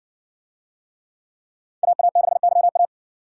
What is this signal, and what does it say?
Morse code, heard as a keyed tone, about 56 wpm, 700 Hz. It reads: AA6YA